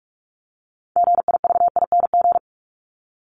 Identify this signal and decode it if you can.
Morse code, heard as a keyed tone, about 44 wpm, 700 Hz. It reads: GIVING
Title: ING Suffix - Top 100 Words with Base Words 45wpm